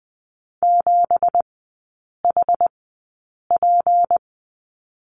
Morse code, heard as a keyed tone, about 20 wpm, spaced as for 8 wpm, 700 Hz. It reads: 7HP
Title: 172 Lesson 28e - Number 9 - Three Characters 20wpm